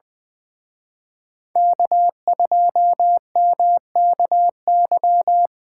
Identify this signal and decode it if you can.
Morse code, heard as a keyed tone, about 23 wpm, 700 Hz. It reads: K2MKY
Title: Speed Racing - USA 1x3 Call Signs 20wpm